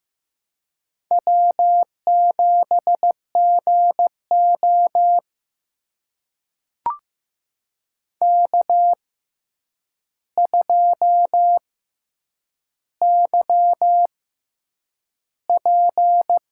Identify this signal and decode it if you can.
Morse code, heard as a keyed tone, about 15 wpm, spaced as for 8 wpm, 700 Hz. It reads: W7GO K2YP
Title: USA Call Signs - 4 Characters - 6x Character Spacing - 15wpm